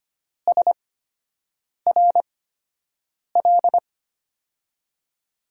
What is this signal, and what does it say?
Morse code, heard as a keyed tone, about 25 wpm, spaced as for 7 wpm, 700 Hz. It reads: SRL
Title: USA Call Signs - 5 Characters - 8x Character Spacing - 25wpm